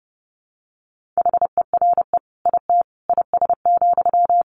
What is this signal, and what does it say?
Morse code, heard as a keyed tone, about 30 wpm, 700 Hz. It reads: HERE IT IS,
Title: Sentences from Top 1500 Words 30wpm